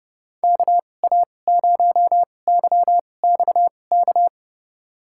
Morse code, heard as a keyed tone, about 32 wpm, spaced as for 22 wpm, 700 Hz. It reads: KA0YXK
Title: USA Call Signs - 6 Characters - 2x Character Spacing - 30wpm